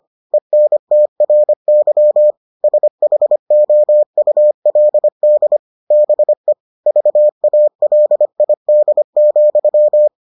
Morse code, heard as a keyed tone, about 25 wpm, 600 Hz. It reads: ENTRY SHOULD BE VALID,